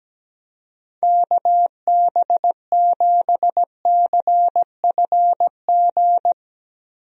Morse code, heard as a keyed tone, about 17 wpm, 700 Hz. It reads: KB7CFG
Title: USA Call Signs - 6 Characters - Repeated 3x - 17wpm